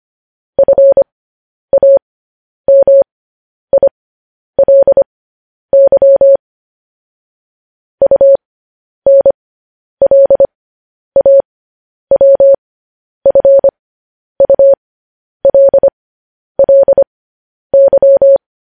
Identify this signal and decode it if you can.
Morse code, heard as a keyed tone, about 25 wpm, spaced as for 10 wpm, 550 Hz. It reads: FAMILY UNLAWFULLY
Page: news headlines in morse code at 10 wpm